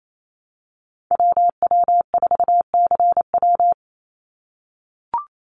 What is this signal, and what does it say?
Morse code, heard as a keyed tone, about 28 wpm, 700 Hz. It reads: WW4CW
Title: USA Call Signs 28wpm